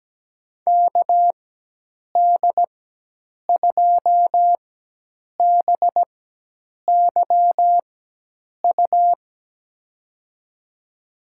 Morse code, heard as a keyed tone, about 17 wpm, spaced as for 8 wpm, 700 Hz. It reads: KD2BYU
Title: USA Call Signs - 4x Character Spacing - 17wpm